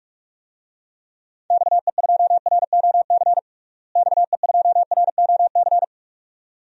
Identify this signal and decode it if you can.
Morse code, heard as a keyed tone, about 46 wpm, 700 Hz. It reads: XE2ROC XE2ROC
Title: DXCC-100 Call Signs - 6 Characters - Repeated 2x - 45wpm